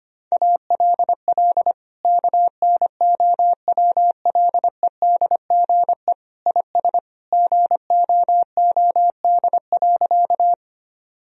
Morse code, heard as a keyed tone, about 25 wpm, 700 Hz. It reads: ALL KNOWLEDGE IS GOOD.
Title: Sentences from Top 1000 Words - Review Part 3 - 25wpm